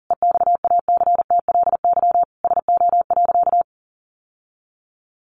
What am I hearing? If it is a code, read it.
Morse code, heard as a keyed tone, about 40 wpm, 700 Hz. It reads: EXACTLY SO.
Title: Sentences from Top 900 Words 40wpm